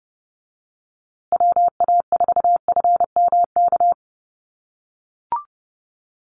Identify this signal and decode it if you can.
Morse code, heard as a keyed tone, about 30 wpm, 700 Hz. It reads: WA4FMK
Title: USA Call Signs Repeated 3x 30wpm